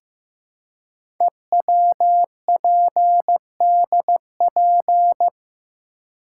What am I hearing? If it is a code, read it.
Morse code, heard as a keyed tone, about 15 wpm, 700 Hz. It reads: EWPDP
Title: Code Groups - 5 Letters - 15wpm